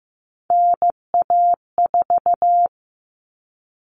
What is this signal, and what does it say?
Morse code, heard as a keyed tone, about 15 wpm, 700 Hz. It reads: NA4
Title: All Cut Numbers - 3-Digit 15wpm